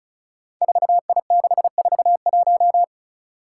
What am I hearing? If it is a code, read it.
Morse code, heard as a keyed tone, about 36 wpm, 700 Hz. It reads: 4I641
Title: Five Challenging Letter-Number Combinations 35wpm